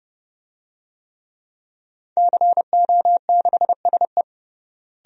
Morse code, heard as a keyed tone, about 31 wpm, 700 Hz. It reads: CO6SE
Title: DXCC-100 Call Signs - 5 Characters - Repeated 2x - 30wpm